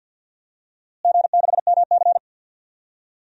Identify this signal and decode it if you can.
Morse code, heard as a keyed tone, about 45 wpm, 700 Hz. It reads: G6MC
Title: DXCC-100 Call Signs - 4 Characters - Repeated 3x - 50wpm